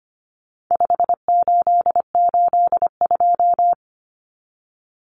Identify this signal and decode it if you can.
Morse code, heard as a keyed tone, about 25 wpm, 700 Hz. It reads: 5882
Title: Numbers - 4-Digit 25wpm